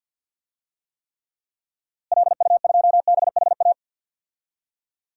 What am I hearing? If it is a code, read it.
Morse code, heard as a keyed tone, about 52 wpm, 700 Hz. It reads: LU2BHA
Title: DXCC-100 Call Signs - 6 Characters - Repeated 2x - 50wpm